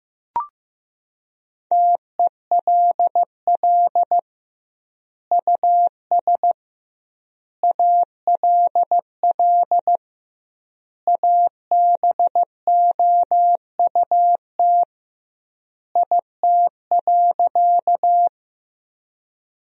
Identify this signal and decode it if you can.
Morse code, heard as a keyed tone, about 16 wpm, 700 Hz. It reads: TELL US ALL ABOUT IT.
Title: 2x Word-Spacing - Sentences from Top 200 Words - 15wpm